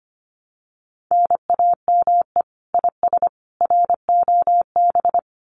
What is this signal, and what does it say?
Morse code, heard as a keyed tone, about 25 wpm, 700 Hz. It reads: NAME IS ROB
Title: Common QSO Phrases 25wpm